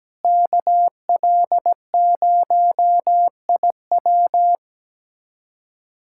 Morse code, heard as a keyed tone, about 17 wpm, 700 Hz. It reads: KL0IW
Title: USA Callsign DE Callsign - 5 Characters - 17wpm